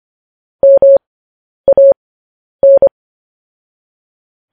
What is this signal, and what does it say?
Morse code, heard as a keyed tone, about 25 wpm, spaced as for 10 wpm, 550 Hz. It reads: MAN
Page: News Headlines in Morse Code at 10 WPM